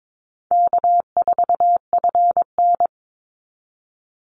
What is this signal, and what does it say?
Morse code, heard as a keyed tone, about 22 wpm, 700 Hz. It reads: K4FN
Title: Easy Call Signs Repeated 3x - 22wpm